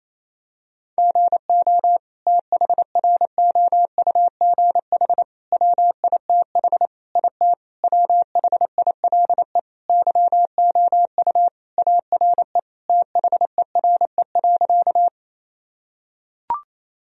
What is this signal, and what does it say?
Morse code, heard as a keyed tone, about 28 wpm, 700 Hz. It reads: GO THROUGH WITH IT WHILE YOU ARE THERE.
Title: Sentences from Top 1000 Words - Review Part 1 - 28wpm